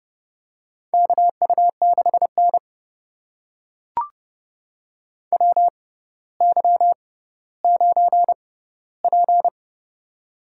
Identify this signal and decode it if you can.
Morse code, heard as a keyed tone, about 29 wpm, spaced as for 15 wpm, 690 Hz. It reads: KU6N WY9P
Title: USA Call Signs - 4 Characters - 6x Character Spacing - 30wpm